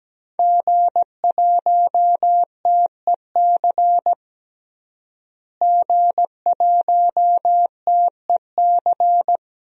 Morse code, heard as a keyed tone, about 17 wpm, 700 Hz. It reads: G1TEC G1TEC
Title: DXCC-100 Call Signs - 5 Characters - Repeated 3x - 17wpm